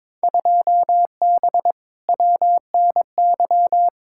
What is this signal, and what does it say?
Morse code, heard as a keyed tone, about 22 wpm, 700 Hz. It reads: 2B WNY
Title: Field Day Exchanges Encore - 22wpm